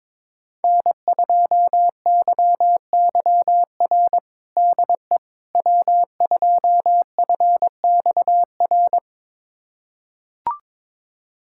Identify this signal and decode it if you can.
Morse code, heard as a keyed tone, about 22 wpm, 700 Hz. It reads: N2YYR DE W2FXR
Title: USA Callsign DE Callsign - 5 Characters - 22wpm